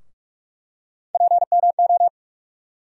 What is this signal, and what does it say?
Morse code, heard as a keyed tone, about 42 wpm, 700 Hz. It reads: PMO
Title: USA Call Sign Postfixes - 3 Letters 45wpm